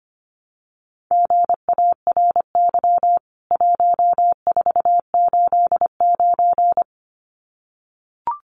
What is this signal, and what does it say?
Morse code, heard as a keyed tone, about 25 wpm, 700 Hz. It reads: GARY 1489
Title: CWT Exchange 25wpm